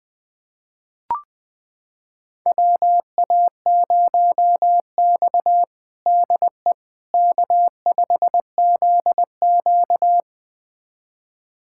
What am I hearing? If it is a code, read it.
Morse code, heard as a keyed tone, about 20 wpm, 700 Hz. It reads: WA0X DE K5ZQ
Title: USA Callsign DE Callsign - 4 Characters - 20wpm